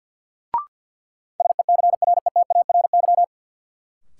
Morse code, heard as a keyed tone, about 50 wpm, 700 Hz. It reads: SECRETARY
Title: Sentences from Top 1300 Words - New Word Drill - 50wpm